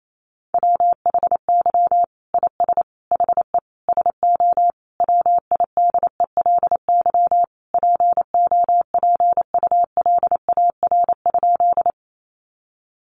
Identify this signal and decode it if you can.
Morse code, heard as a keyed tone, about 28 wpm, 700 Hz. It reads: WHY IS HE SO WIDELY POPULAR?
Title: Sentences from Top 1200 Words 28wpm